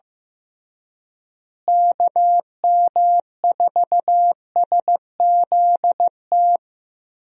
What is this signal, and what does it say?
Morse code, heard as a keyed tone, about 15 wpm, 700 Hz. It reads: KM4SZT
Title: USA Call Signs - 6 Characters - Repeated 3x - 15wpm